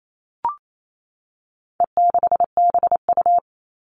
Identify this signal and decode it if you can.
Morse code, heard as a keyed tone, about 28 wpm, 700 Hz. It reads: E6BU
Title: All Cut Numbers - 4-Digit 28wpm